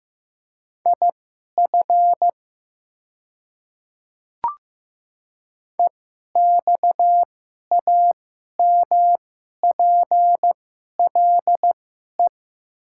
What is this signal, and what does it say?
Morse code, heard as a keyed tone, about 15 wpm, spaced as for 11 wpm, 700 Hz. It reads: IF EXAMPLE
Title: 2x Character Spacing - Top 500 Words - 15wpm